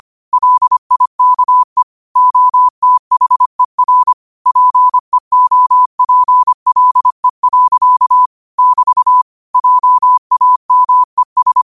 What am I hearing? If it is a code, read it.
Morse code, heard as a keyed tone, about 25 wpm, 1000 Hz. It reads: LIKE OTHER PEOPLE. <BT> JAMES
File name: qotdfiles / 25wpm